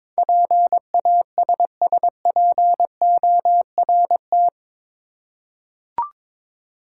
Morse code, heard as a keyed tone, about 22 wpm, 700 Hz. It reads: PASSPORT
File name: practice / 22wpm / Eight Letter Words - 22wpm